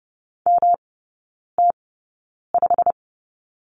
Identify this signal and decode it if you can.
Morse code, heard as a keyed tone, about 30 wpm, spaced as for 9 wpm, 700 Hz. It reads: MT5
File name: Lesson 18e - Letter m - Three Characters 30wpm